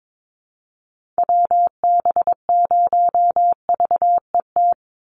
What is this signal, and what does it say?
Morse code, heard as a keyed tone, about 22 wpm, 700 Hz. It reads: WB0VET
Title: USA Call Signs 22wpm